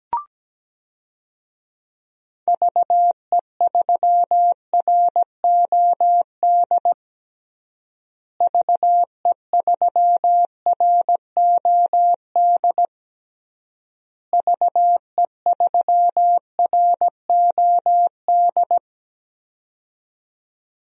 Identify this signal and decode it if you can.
Morse code, heard as a keyed tone, about 17 wpm, 700 Hz. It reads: VE3ROD VE3ROD VE3ROD
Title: DXCC-100 Call Signs - 6 Characters - Repeated 3x - 17wpm